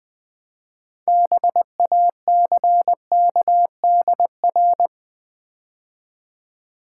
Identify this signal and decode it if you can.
Morse code, heard as a keyed tone, about 20 wpm, 700 Hz. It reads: BACKDR
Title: Build A Word - 2 Letters At A Time - 8-Letter Words - 20wpm